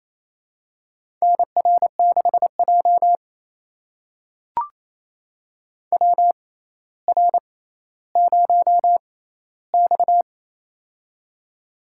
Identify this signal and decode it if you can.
Morse code, heard as a keyed tone, about 28 wpm, spaced as for 14 wpm, 700 Hz. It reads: NR6J WR0X
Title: USA Call Signs - 4 Characters - 6x Character Spacing - 28wpm